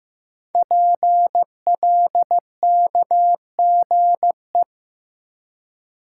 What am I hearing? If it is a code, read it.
Morse code, heard as a keyed tone, about 15 wpm, 700 Hz. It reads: PLKGE
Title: Code Groups - 5 Letters - 15wpm